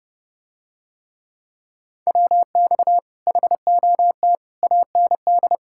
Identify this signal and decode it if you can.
Morse code, heard as a keyed tone, about 30 wpm, 700 Hz. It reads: WX HOT AND